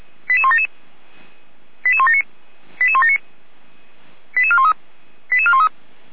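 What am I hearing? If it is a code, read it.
Five-tone selective call (ZVEI), heard as quick runs of five tones, tones of about 70 ms. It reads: 88177 88179 88179 88413 88413
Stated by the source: ZVEI five-tone selective call 88171 received on SDR.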